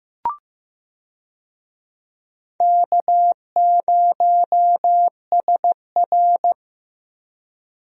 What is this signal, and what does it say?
Morse code, heard as a keyed tone, about 15 wpm, 700 Hz. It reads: K0SR
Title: USA Call Signs - 4 Characters - Repeated 2x - 15wpm